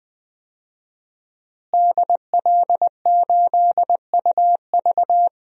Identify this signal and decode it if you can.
Morse code, heard as a keyed tone, about 20 wpm, 700 Hz. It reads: DL8UV
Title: DXCC-100 Call Signs - 5 Characters - Repeated 2x - 20wpm